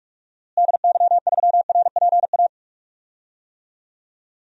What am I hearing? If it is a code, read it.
Morse code, heard as a keyed tone, about 44 wpm, 700 Hz. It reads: DY3RPA
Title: DXCC-100 Call Signs - 6 Characters - Repeated 2x - 45wpm